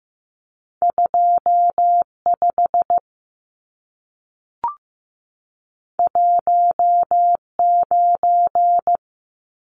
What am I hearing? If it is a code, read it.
Morse code, heard as a keyed tone, about 15 wpm, 700 Hz. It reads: 25 19
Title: Numbers - 2-Digit 15wpm